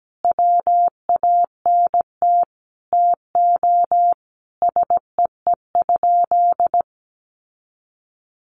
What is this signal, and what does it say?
Morse code, heard as a keyed tone, about 17 wpm, 700 Hz. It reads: WANT TO SEE?
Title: Sentences from Top 1100 Words 17wpm